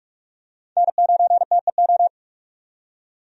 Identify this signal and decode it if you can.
Morse code, heard as a keyed tone, about 42 wpm, 700 Hz. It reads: N9TEO
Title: USA Call Signs - 5 Characters - 45wpm